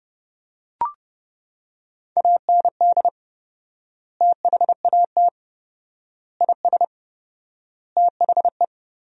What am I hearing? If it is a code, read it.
Morse code, heard as a keyed tone, about 32 wpm, 700 Hz. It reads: AND THAT IS THE
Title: 4x Word-Spacing - Sentences from Top 400 Words - 30wpm